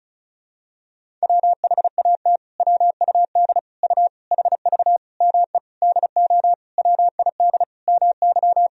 Morse code, heard as a keyed tone, about 36 wpm, 700 Hz. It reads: WHAT WUD U HV ME DO WID MY
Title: Sentences from Top 300 Words and Abbreviations 35wpm